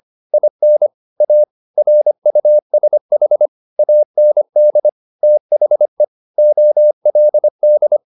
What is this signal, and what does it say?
Morse code, heard as a keyed tone, about 25 wpm, 600 Hz. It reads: IN A RUSH AND THE OLD